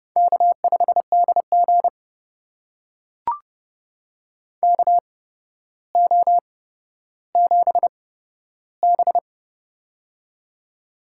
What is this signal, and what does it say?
Morse code, heard as a keyed tone, about 30 wpm, spaced as for 13 wpm, 700 Hz. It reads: K5DG KO7B
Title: USA Call Signs - 4 Characters - 8x Character Spacing - 30wpm